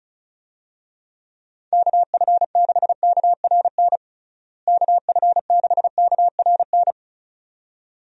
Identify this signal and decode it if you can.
Morse code, heard as a keyed tone, about 36 wpm, 700 Hz. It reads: KF6KRN KF6KRN
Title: USA Call Signs - 6 Characters - Repeated 2x - 35wpm